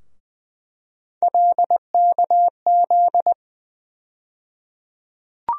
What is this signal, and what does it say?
Morse code, heard as a keyed tone, about 20 wpm, 700 Hz. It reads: LKZ
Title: USA Call Sign Postfixes - 3 Letters 20wpm